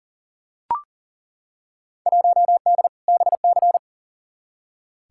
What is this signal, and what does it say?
Morse code, heard as a keyed tone, about 40 wpm, 700 Hz. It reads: 1D BC
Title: Field Day Exchanges - 40wpm